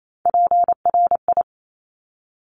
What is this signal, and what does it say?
Morse code, heard as a keyed tone, about 28 wpm, 700 Hz. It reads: PRI
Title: Top Three Letter Combinations in Words 28wpm